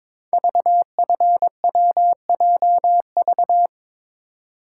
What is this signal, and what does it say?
Morse code, heard as a keyed tone, about 22 wpm, 690 Hz. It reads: VFWJV
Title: Code Groups - 5 Letters - 22wpm